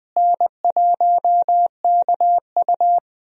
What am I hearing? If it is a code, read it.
Morse code, heard as a keyed tone, about 20 wpm, 700 Hz. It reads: N1KU
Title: USA Call Signs - 4 Characters - Repeated 2x - 20wpm